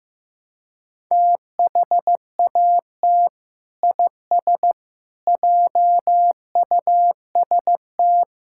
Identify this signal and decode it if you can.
Morse code, heard as a keyed tone, about 15 wpm, 700 Hz. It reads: THAT IS JUST
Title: Sentences from Top 1600 Words 15wpm